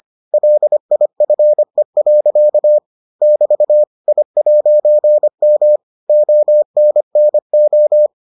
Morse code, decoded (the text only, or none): LIFE. <BT> I'M ONNO